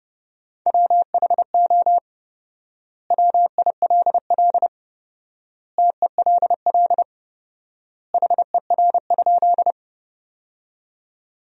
Morse code, heard as a keyed tone, about 32 wpm, 700 Hz. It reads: WHO WILL TELL HER?
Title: 4x Word-Spacing - Sentences from Top 200 Words - 30wpm